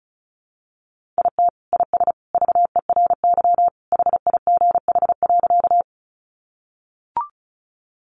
Morse code, heard as a keyed tone, about 34 wpm, 700 Hz. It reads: IT IS VERY HIGH.
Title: Sentences from Top 200 Words 35wpm